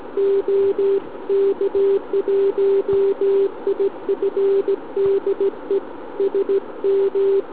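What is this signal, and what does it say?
Morse code, heard as a keyed tone, about 14 wpm, 400 Hz. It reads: OK1IF DE SM
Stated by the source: SM7ALC heard via tropo on 144 MHz with an FT817 and a quarter-wave antenna on a central-heating radiator.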